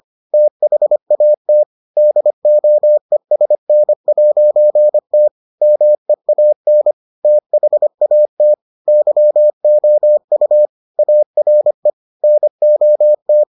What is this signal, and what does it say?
Morse code, heard as a keyed tone, about 25 wpm, 600 Hz. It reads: THAT DOESN'T MEAN THAT YOU ARE NOT